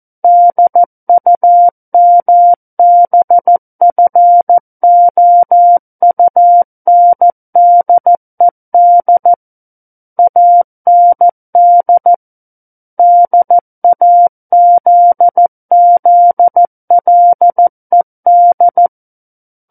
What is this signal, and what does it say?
Morse code, heard as a keyed tone, about 14 wpm, 700 Hz. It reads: DUMBFOUNDED AND DAZZLED
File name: Book One / 05-Chapter 5 - 14 WPM